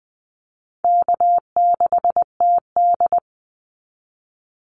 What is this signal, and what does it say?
Morse code, heard as a keyed tone, about 19 wpm, 700 Hz. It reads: K6TD
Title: Call Signs Repeated 3x - 20wpm